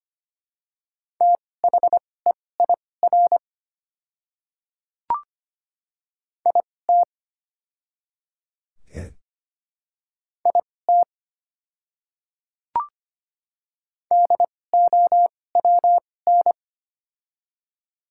Morse code, heard as a keyed tone, about 25 wpm, spaced as for 18 wpm, 700 Hz. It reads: THEIR IT IT DOWN
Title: 2x Character Spacing - Top 200 Words - 25wpm